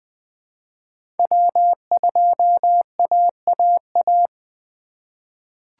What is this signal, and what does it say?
Morse code, heard as a keyed tone, about 20 wpm, 700 Hz. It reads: W2AAA
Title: USA Call Signs - 2x Character Spacing - 20wpm